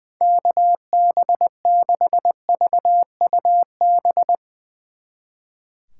Morse code, heard as a keyed tone, about 22 wpm, 700 Hz. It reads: KB6VUB
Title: Speed Racing - USA 2x3 Call Signs 20wpm